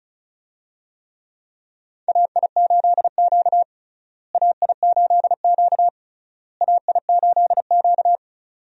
Morse code, heard as a keyed tone, about 36 wpm, 700 Hz. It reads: AI8Q AI8Q AI8Q
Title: USA Call Signs - 4 Characters - Repeated 3x - 35wpm